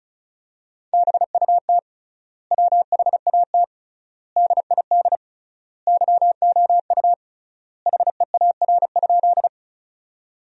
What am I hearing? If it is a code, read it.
Morse code, heard as a keyed tone, about 36 wpm, 700 Hz. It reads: BUT WHAT DID YOU HEAR?
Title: 3x Word-Spacing - Sentences from Top 300 Words - 35wpm